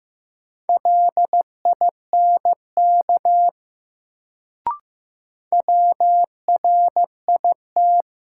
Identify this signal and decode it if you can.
Morse code, heard as a keyed tone, about 15 wpm, 700 Hz. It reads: LINK WRIT
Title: Sentences from Top 1500 Words - New Word Drill Alt - 15wpm